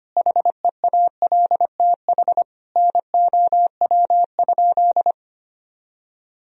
Morse code, heard as a keyed tone, about 25 wpm, 700 Hz. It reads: HEALTH NOW?
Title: Sentences from Top 1000 Words - Review Part 3 - 25wpm